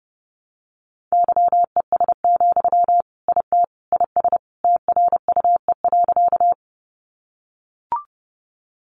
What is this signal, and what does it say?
Morse code, heard as a keyed tone, about 30 wpm, 700 Hz. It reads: YES, IT IS TRUE.
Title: Sentences from Top 500 Words 30wpm